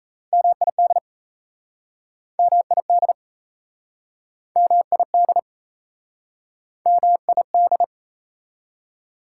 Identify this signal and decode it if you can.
Morse code, heard as a keyed tone, about 38 wpm, 690 Hz. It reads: MID MID MID MID